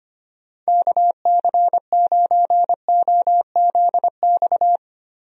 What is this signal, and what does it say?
Morse code, heard as a keyed tone, about 25 wpm, 700 Hz. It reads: KC9OZX
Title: USA Call Signs - 6 Characters - Repeated 2x - 25wpm